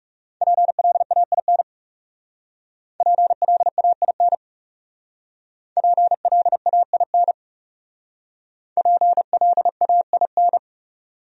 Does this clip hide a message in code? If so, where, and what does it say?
Yes: Morse code from 0.4 to 10.6 s, heard as a keyed tone, about 38 wpm, 700 Hz. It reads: PLAIN PLAIN PLAIN PLAIN